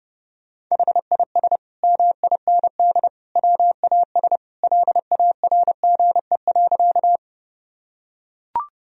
Morse code, heard as a keyed tone, about 30 wpm, 700 Hz. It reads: HIS MIND WAS LARGE.
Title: Sentences from Top 1000 Words - Review Part 1 - 30wpm